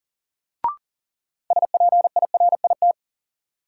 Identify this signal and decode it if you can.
Morse code, heard as a keyed tone, about 40 wpm, 700 Hz. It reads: SPIRIT
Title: Sentences from Top 1400 Words - New Word Drill - 40wpm